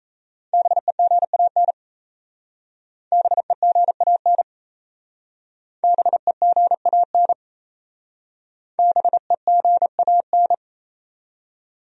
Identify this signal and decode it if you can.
Morse code, heard as a keyed tone, about 37 wpm, 700 Hz. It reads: BEGAN BEGAN BEGAN BEGAN